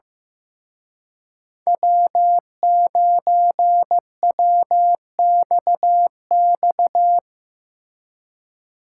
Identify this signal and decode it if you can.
Morse code, heard as a keyed tone, about 17 wpm, 700 Hz. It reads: W9WXX
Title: Speed Racing - USA 1x3 Call Signs 15wpm